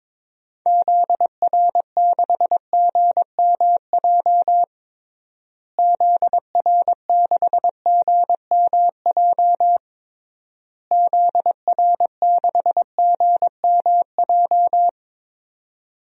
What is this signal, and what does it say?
Morse code, heard as a keyed tone, about 22 wpm, 700 Hz. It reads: ZR6GMJ ZR6GMJ ZR6GMJ
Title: DXCC-100 Call Signs - 6 Characters - Repeated 3x - 22wpm